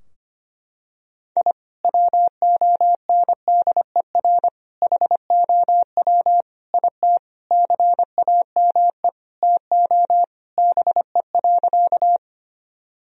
Morse code, heard as a keyed tone, about 25 wpm, 700 Hz. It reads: I WONDER HOW IT CAME TO BE.
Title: Sentences from Top 1000 Words - Review Part 3 - 25wpm